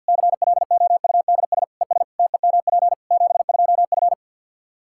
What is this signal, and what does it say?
Morse code, heard as a keyed tone, about 51 wpm, 700 Hz. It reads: CLOUDS ES TEMP 72F
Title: Practice QSO - Exchange 2 - Info - 50wpm